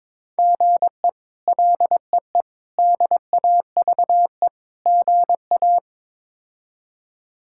Morse code, heard as a keyed tone, about 22 wpm, 700 Hz. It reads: GE LEE DAVE GA
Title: K1USN SST QSO Practice - 22wpm